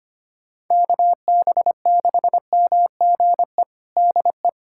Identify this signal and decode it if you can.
Morse code, heard as a keyed tone, about 25 wpm, 700 Hz. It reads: KB6MGE DE
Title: USA Callsign DE Callsign 25wpm